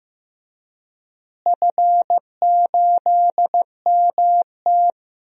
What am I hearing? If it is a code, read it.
Morse code, heard as a keyed tone, about 15 wpm, 700 Hz. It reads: F8MT
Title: DXCC-100 Call Signs - 4 Characters - Repeated 2x - 15wpm